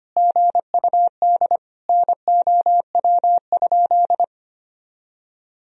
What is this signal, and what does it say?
Morse code, heard as a keyed tone, about 25 wpm, 700 Hz. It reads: GUD NOW?
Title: Sentences from Top 200 Words and Abbreviations 25wpm